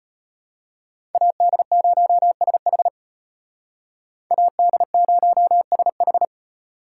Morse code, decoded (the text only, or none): AD0SH AD0SH